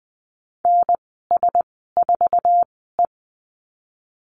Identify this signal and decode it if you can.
Morse code, heard as a keyed tone, about 19 wpm, spaced as for 14 wpm, 700 Hz. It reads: NS4E
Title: 055 Lesson 10g - Letter r - Call Signs 20wpm